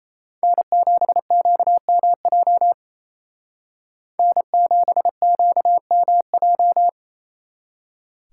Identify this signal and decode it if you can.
Morse code, heard as a keyed tone, about 33 wpm, 700 Hz. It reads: N7QMJ N7QMJ
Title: Speed Racing - USA 1x3 Call Signs 28wpm